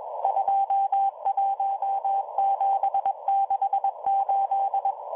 Morse code, heard as a keyed tone, about 22 wpm, 790 Hz, with radio noise. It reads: 21768